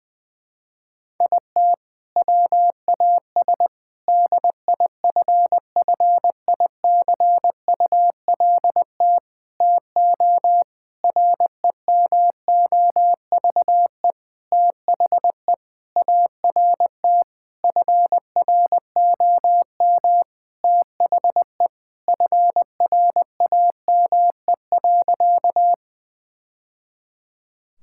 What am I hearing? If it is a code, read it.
Morse code, heard as a keyed tone, about 20 wpm, 700 Hz. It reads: IT WAS DIFFICULT TO REMOVE THE ART FROM THE FRAME.
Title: Sentences from Top 1000 Words - Review Part 6 - 20wpm